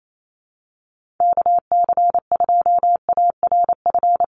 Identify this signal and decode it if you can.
Morse code, heard as a keyed tone, about 28 wpm, 700 Hz. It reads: KC2ARF
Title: USA Call Signs Repeated 2x 28wpm